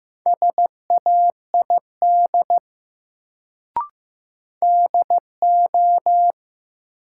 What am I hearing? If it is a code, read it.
Morse code, heard as a keyed tone, about 15 wpm, 700 Hz. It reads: SAID DO
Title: Sentences from Top 100 Words - New Word Drill - 15wpm